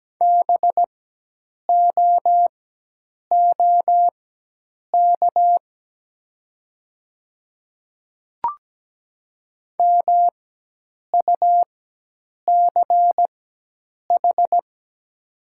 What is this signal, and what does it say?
Morse code, heard as a keyed tone, about 17 wpm, spaced as for 8 wpm, 700 Hz. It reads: BOOK MUCH
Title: 4x Character Spacing - Top 300 Words - 17wpm